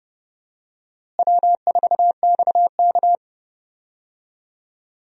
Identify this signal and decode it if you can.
Morse code, heard as a keyed tone, about 29 wpm, 700 Hz. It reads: W4XK
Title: USA Call Signs - 4 Characters - Repeated 3x - 30wpm